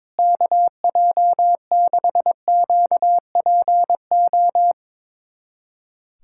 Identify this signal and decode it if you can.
Morse code, heard as a keyed tone, about 22 wpm, 700 Hz. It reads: KJ6QPO
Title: USA Callsign DE Callsign 22wpm